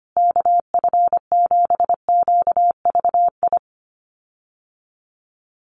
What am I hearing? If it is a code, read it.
Morse code, heard as a keyed tone, about 25 wpm, 700 Hz. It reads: KF7QVI
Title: USA Call Signs Repeated 3x 25wpm